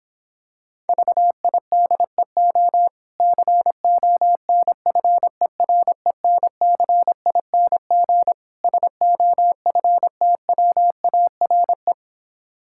Morse code, heard as a keyed tone, about 26 wpm, 700 Hz. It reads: VIDEO CONFERENCING SOFTWARE